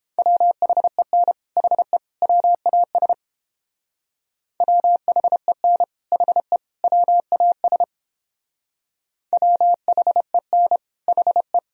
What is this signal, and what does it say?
Morse code, heard as a keyed tone, about 30 wpm, 700 Hz. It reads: WHEN HE WAS WHEN HE WAS WHEN HE